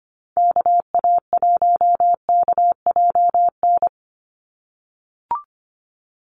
Morse code, heard as a keyed tone, about 25 wpm, 700 Hz. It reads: KA1KJN
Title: USA Call Signs 25wpm